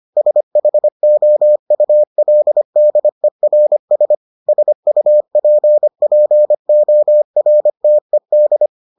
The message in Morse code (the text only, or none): SHOULDERS SUPPORTED